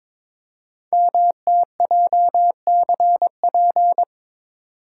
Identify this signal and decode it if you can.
Morse code, heard as a keyed tone, about 22 wpm, 700 Hz. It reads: MTJCP
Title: Code Groups - 5 Letters - 22wpm